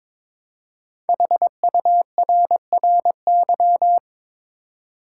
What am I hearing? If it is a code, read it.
Morse code, heard as a keyed tone, about 22 wpm, 700 Hz. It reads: HURRY
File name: 5 Letter Words - First 3 Letters Given - 22wpm